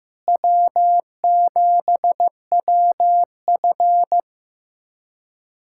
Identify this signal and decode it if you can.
Morse code, heard as a keyed tone, about 15 wpm, 700 Hz. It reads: W7WF
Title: USA Callsign DE Callsign - 4 Characters - 15wpm